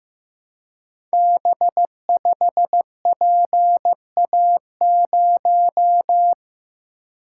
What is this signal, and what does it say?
Morse code, heard as a keyed tone, about 15 wpm, 700 Hz. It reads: B5PA0
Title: Code Groups - 5 Letter:Numbers - 15wpm